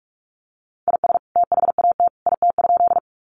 Morse code, heard as a keyed tone, about 44 wpm, 700 Hz. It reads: IS THAT IT?
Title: Sentences from Top 100 Words 45wpm